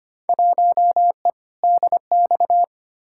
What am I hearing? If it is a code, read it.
Morse code, heard as a keyed tone, about 25 wpm, 700 Hz. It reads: 1E DX
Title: Field Day Exchanges Encore - 25wpm